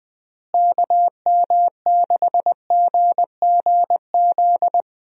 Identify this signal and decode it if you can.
Morse code, heard as a keyed tone, about 20 wpm, 700 Hz. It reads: KM6GGZ
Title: USA Call Signs - 6 Characters - Repeated 2x - 20wpm